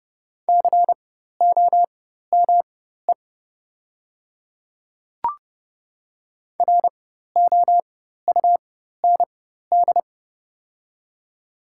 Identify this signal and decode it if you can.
Morse code, heard as a keyed tone, about 30 wpm, spaced as for 14 wpm, 690 Hz. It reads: COME ROUND
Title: 4x Character Spacing - Top 200 Words - 30wpm